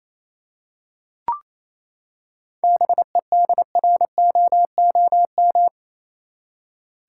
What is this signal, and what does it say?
Morse code, heard as a keyed tone, about 28 wpm, 700 Hz. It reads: BEDROOM
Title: Seven Letter Words - 28wpm